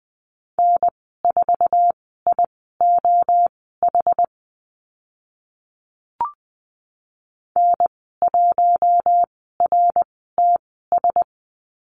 Morse code, heard as a keyed tone, about 20 wpm, spaced as for 14 wpm, 700 Hz. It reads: N4IOH N1RTS
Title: 069 Lesson 12g - Letter d - Call Signs 20wpm